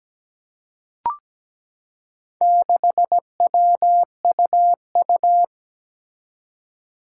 Morse code, heard as a keyed tone, about 17 wpm, 700 Hz. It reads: 6WUU
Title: Four Challenging Letter-Number Combinations 17wpm